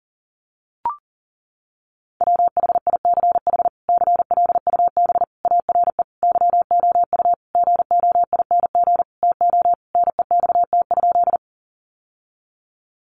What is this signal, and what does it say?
Morse code, heard as a keyed tone, about 40 wpm, 700 Hz. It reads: WHICH CLUB ARE YOU GOING TO NEXT?
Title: Sentences from Top 1200 Words 40wpm